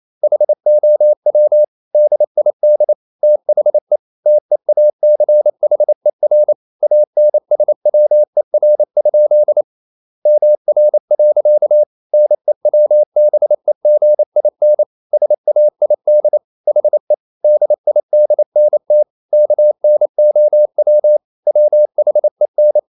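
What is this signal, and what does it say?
Morse code, heard as a keyed tone, about 28 wpm, 600 Hz. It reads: HOW DID THE TEACHER ANSWER? MR. NEWBEGIN SAID HE DIDNT KNOW WHEN
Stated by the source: Morse code practice at 28 wpm, text of Ken Follett's novel Never in English, from CW med Gnister.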